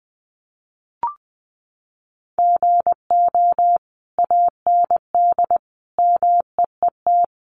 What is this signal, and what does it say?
Morse code, heard as a keyed tone, about 19 wpm, 700 Hz. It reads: GO AND MEET
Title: Sentences from Top 1100 Words 20wpm